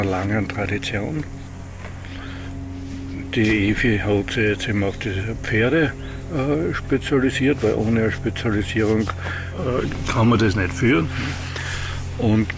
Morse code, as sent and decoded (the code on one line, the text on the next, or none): none